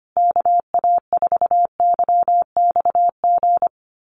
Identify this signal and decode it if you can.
Morse code, heard as a keyed tone, about 25 wpm, 700 Hz. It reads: KA4YXG
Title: USA Call Signs Repeated 2x 25wpm